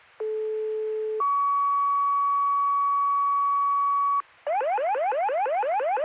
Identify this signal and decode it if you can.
Two-tone page, heard as a one-second tone, then a three-second tone, about 430 then 1130 Hz. It is heard from 0.2 to 4.2 s.
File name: Lakeland EMS Duty